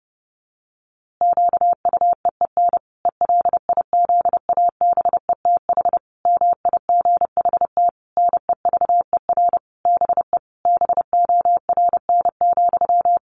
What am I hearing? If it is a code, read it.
Morse code, heard as a keyed tone, about 30 wpm, 700 Hz. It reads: QUEEN ELIZABETH MIGHT NEVER BE BORN,
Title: A Sound Of Thunder 30wpm